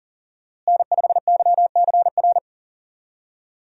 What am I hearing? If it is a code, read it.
Morse code, heard as a keyed tone, about 40 wpm, 700 Hz. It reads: N5YCR
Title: USA Call Signs - 5 Characters - Repeated 2x - 40wpm